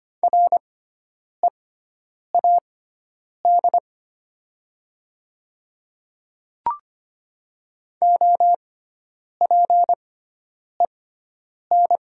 Morse code, heard as a keyed tone, about 25 wpm, spaced as for 9 wpm, 700 Hz. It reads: READ OPEN
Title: 6x Character Spacing - Top 300 Words - 25wpm